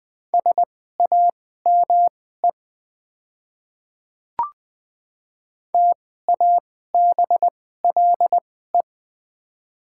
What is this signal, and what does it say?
Morse code, heard as a keyed tone, about 21 wpm, spaced as for 15 wpm, 690 Hz. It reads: SAME TABLE
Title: 2x Character Spacing - Top 500 Words - 20wpm